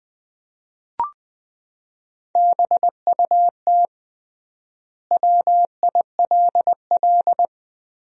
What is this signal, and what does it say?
Morse code, heard as a keyed tone, about 21 wpm, 700 Hz. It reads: BUT WILL
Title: 3x Word-Spacing - Sentences from Top 100 Words - 20wpm